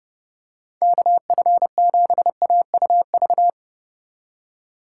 Morse code, heard as a keyed tone, about 30 wpm, 700 Hz. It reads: KF7AUV